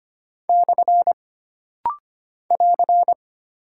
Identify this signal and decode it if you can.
Morse code, heard as a keyed tone, about 25 wpm, 700 Hz. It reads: / <AR>
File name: Single Punctuation-Prosign - Rapid-Fire - 25wpm